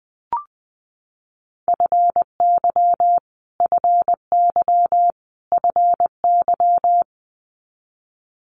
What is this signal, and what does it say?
Morse code, heard as a keyed tone, about 20 wpm, 700 Hz. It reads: FY FY FY
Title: Multi-letter Phonemes Repeated 3x 20wpm